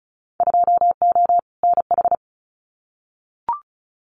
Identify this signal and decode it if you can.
Morse code, heard as a keyed tone, about 34 wpm, 700 Hz. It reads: 2O NH
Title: Winter Field Day Exchanges 35wpm